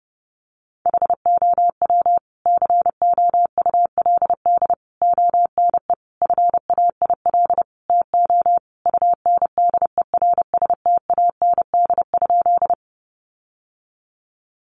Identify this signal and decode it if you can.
Morse code, heard as a keyed tone, about 30 wpm, 700 Hz. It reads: HOW COULD ONE FAIL TO UNDERSTAND?
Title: Sentences from Top 1100 Words 30wpm